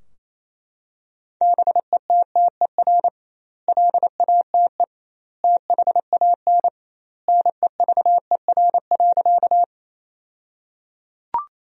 Morse code, heard as a keyed tone, about 28 wpm, 700 Hz. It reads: BETTER LATE THAN NEVER.
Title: 2x Word-Spacing - Sentences from Top 400 Words - 28wpm